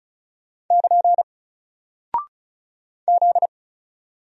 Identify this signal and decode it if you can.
Morse code, heard as a keyed tone, about 36 wpm, 700 Hz. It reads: <KN> Z
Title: Single Character (Letter:Number:Punctuation:Prosign) Alt 35wpm